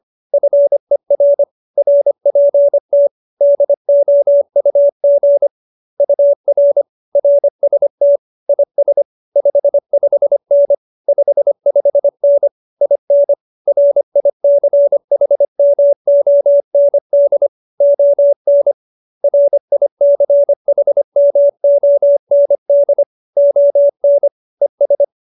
Morse code, heard as a keyed tone, about 25 wpm, 570 Hz. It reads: FER RPT DOUG UR RST IS 55N 55N IN RICHMOND ON RICHMOND ON ES